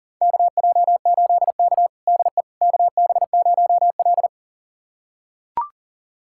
Morse code, heard as a keyed tone, about 40 wpm, 700 Hz. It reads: KJ8K DE KB0L
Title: USA Callsign DE Callsign - 4 Characters - 40wpm